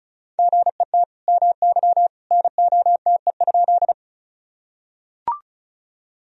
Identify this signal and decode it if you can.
Morse code, heard as a keyed tone, about 36 wpm, 700 Hz. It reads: GET MY NOTE?
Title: Sentences from Top 1000 Words - Review Part 2 - 35wpm